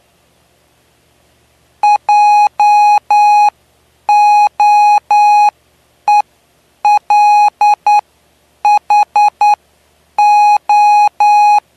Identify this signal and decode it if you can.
Morse code, heard as a keyed tone, about 9 wpm, spaced as for 8 wpm, 830 Hz. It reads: JOELHO
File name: cw / py2lcd / 09-PY2LCD-Li��o 9